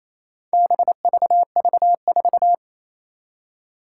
Morse code, heard as a keyed tone, about 28 wpm, 700 Hz. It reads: BVV4
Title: Four Challenging Letter-Number Combinations 28wpm